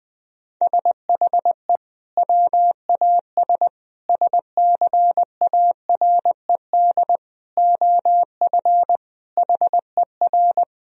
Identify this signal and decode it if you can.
Morse code, heard as a keyed tone, about 20 wpm, 700 Hz. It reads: SHE WAS SCARED OF HER
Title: Sentences from Top 1000 Words - Review Part 4 - 20wpm